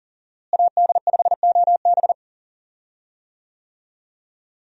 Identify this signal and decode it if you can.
Morse code, heard as a keyed tone, about 40 wpm, 700 Hz. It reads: AD5OB
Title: USA Call Signs - 5 Characters - Repeated 2x - 40wpm